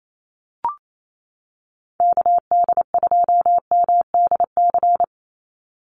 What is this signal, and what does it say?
Morse code, heard as a keyed tone, about 28 wpm, 700 Hz. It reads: KD2MDC
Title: USA Call Signs 28wpm